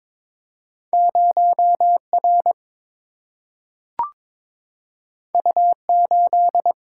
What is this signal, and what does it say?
Morse code, heard as a keyed tone, about 22 wpm, 700 Hz. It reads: 0 R U 8
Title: Single-Letter Single-Number 22wpm